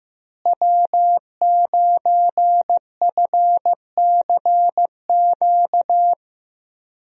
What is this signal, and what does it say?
Morse code, heard as a keyed tone, about 15 wpm, 700 Hz. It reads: W9FCQ
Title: USA Call Signs - 5 Characters - Repeated 3x - 15wpm